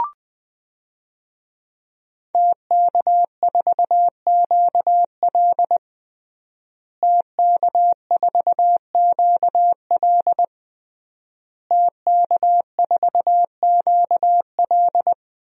Morse code, heard as a keyed tone, about 21 wpm, 700 Hz. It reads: TK4QL TK4QL TK4QL
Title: DXCC-100 Call Signs - 5 Characters - Repeated 3x - 20wpm